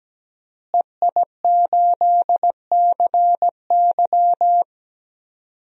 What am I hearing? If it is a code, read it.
Morse code, heard as a keyed tone, about 17 wpm, 700 Hz. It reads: EI8CY
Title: DXCC-100 Call Signs - 5 Characters - Repeated 3x - 17wpm